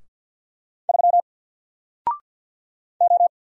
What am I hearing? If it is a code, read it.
Morse code, heard as a keyed tone, about 57 wpm, 700 Hz. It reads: 3 O
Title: Single Letter-Number Alt 50wpm